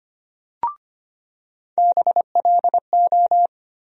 Morse code, heard as a keyed tone, about 25 wpm, 700 Hz. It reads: BLO
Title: USA Call Sign Postfixes - 3 Letters 25wpm